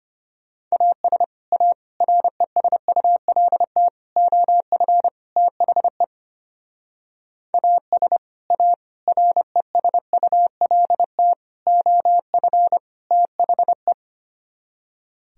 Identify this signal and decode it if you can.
Morse code, heard as a keyed tone, about 30 wpm, 700 Hz. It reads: AS A RESULT OF THE AS A RESULT OF THE
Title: Speed-Racing - Sets of 5 Words 25wpm